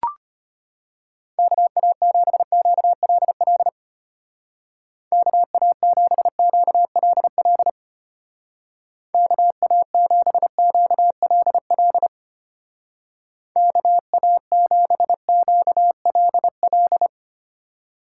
Morse code, heard as a keyed tone, about 32 wpm, 700 Hz. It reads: KA7QLL KA7QLL KA7QLL KA7QLL